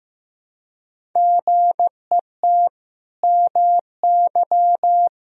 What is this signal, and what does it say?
Morse code, heard as a keyed tone, about 15 wpm, 700 Hz. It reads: GET MY
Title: Sentences from Top 1000 Words - Review Part 1 - 15wpm